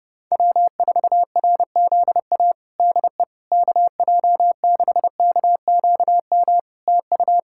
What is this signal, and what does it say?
Morse code, heard as a keyed tone, about 30 wpm, 700 Hz. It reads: W4RZA DE KJ6KQM TU
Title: Practice QSO - Exchange 4 - Ending - 30wpm